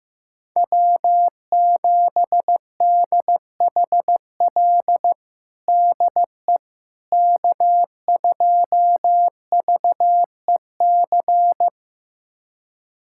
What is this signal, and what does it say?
Morse code, heard as a keyed tone, about 15 wpm, 700 Hz. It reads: W7DHL DE K2VEC
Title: USA Callsign DE Callsign - 5 Characters - 15wpm